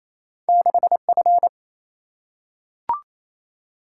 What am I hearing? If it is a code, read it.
Morse code, heard as a keyed tone, about 28 wpm, 700 Hz. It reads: 6F
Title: Single-Letter Single-Number 28wpm